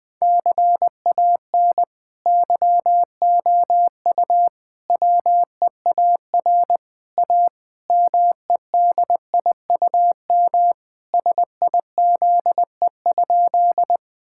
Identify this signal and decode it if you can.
Morse code, heard as a keyed tone, about 20 wpm, 700 Hz. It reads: CAN YOU WEAR A MEDIUM SIZE?
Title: Sentences from Top 1000 Words - Review Part 6 - 20wpm